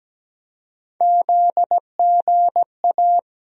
Morse code, heard as a keyed tone, about 17 wpm, 700 Hz. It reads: ZGA
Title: USA Call Sign Postfixes - 3 Letters 17wpm